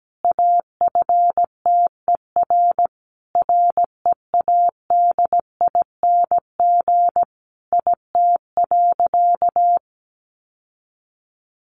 Morse code, heard as a keyed tone, about 17 wpm, 700 Hz. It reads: AFTER READING IT.
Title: Sentences from Top 1300 Words 17wpm